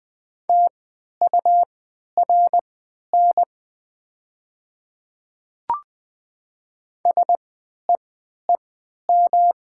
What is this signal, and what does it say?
Morse code, heard as a keyed tone, about 20 wpm, spaced as for 11 wpm, 700 Hz. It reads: TURN SEEM
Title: 3x Character Spacing - Top 300 Words - 20wpm